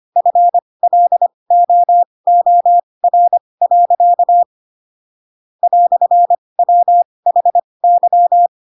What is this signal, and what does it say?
Morse code, heard as a keyed tone, about 25 wpm, spaced as for 20 wpm, 700 Hz. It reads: FLOOR. "WHY
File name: Pinocchio / Pinocchio ch7 15 25 ik1hge